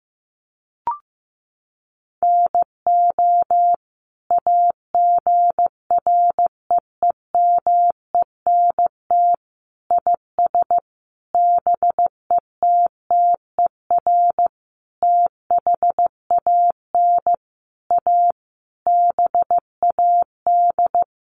Morse code, heard as a keyed tone, about 15 wpm, 700 Hz. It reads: NO AGREEMENT IS BETTER THAN A BAD
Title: Sentences from Top 1500 Words 15wpm